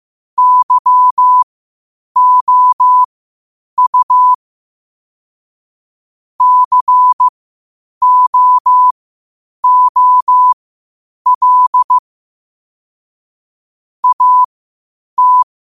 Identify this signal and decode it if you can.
Morse code, heard as a keyed tone, about 15 wpm, spaced as for 8 wpm, 1000 Hz. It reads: YOU COOL AT
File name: qotdfiles / 8wpm